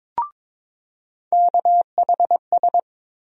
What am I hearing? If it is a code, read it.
Morse code, heard as a keyed tone, about 22 wpm, 700 Hz. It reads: KHS
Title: USA Call Sign Postfixes - 3 Letters 22wpm